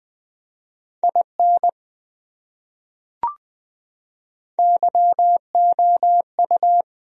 Morse code, heard as a keyed tone, about 20 wpm, 700 Hz. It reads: IN YOU
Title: Top 100 Words Repeated 20wpm